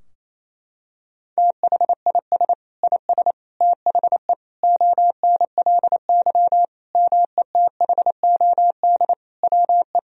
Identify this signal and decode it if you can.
Morse code, heard as a keyed tone, about 28 wpm, 700 Hz. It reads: THIS IS THE ONLY METHOD WE
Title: Sentences from Top 1000 Words - Review Part 3 - 28wpm